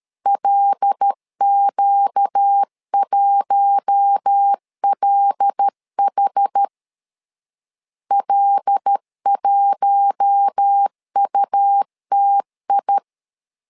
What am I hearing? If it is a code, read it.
Morse code, heard as a keyed tone, about 13 wpm, 800 Hz. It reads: LQ1LH L1UTI
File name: LEZIONI / 5 LBVC / 5 L 12wpm 1913